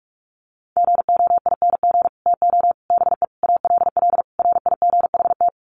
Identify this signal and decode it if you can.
Morse code, heard as a keyed tone, about 44 wpm, 700 Hz. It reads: GOING TO BE ALL RIGHT
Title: Sets of 5 Words Encore2 45wpm